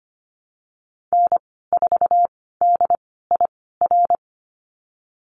Morse code, heard as a keyed tone, about 25 wpm, spaced as for 16 wpm, 700 Hz. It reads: N4DIR
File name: Lesson 15g - Number 5 - Call Signs 25wpm